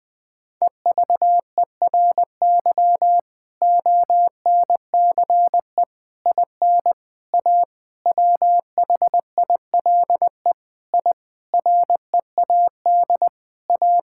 Morse code, heard as a keyed tone, about 20 wpm, 700 Hz. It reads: EVERY ONCE IN A WHILE I READ A
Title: Sentences with 5-Word Phrases - Spoken Beginning - 20wpm